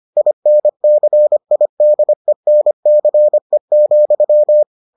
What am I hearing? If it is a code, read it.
Morse code, heard as a keyed tone, about 25 wpm, 600 Hz. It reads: INCIDENCE,